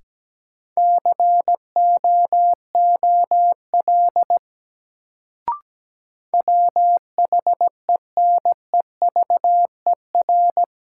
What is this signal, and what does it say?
Morse code, heard as a keyed tone, about 17 wpm, 700 Hz. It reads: COOL WHENEVER
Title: Sentences from Top 1200 Words - New Word Drill Alt - 17wpm